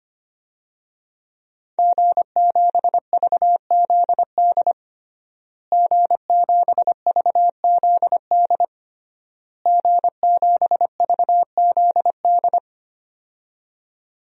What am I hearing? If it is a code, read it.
Morse code, heard as a keyed tone, about 25 wpm, 700 Hz. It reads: G7VZD G7VZD G7VZD
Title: DXCC-100 Call Signs - 5 Characters - Repeated 3x - 25wpm